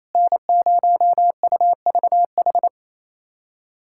Morse code, heard as a keyed tone, about 28 wpm, 700 Hz. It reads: N0UVH
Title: USA Callsign DE Callsign - 5 Characters - 28wpm